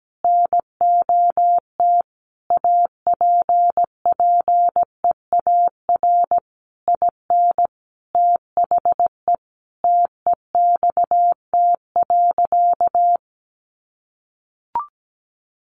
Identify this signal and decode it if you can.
Morse code, heard as a keyed tone, about 17 wpm, 700 Hz. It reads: NOT APPEAR IN THE TEXT.
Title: Sentences from Top 1200 Words 17wpm